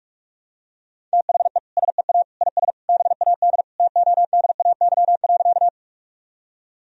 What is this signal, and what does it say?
Morse code, heard as a keyed tone, about 46 wpm, 700 Hz. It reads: THE SEA IS BAD TODAY.
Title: Sentences from Top 1000 Words - Review Part 1 - 45wpm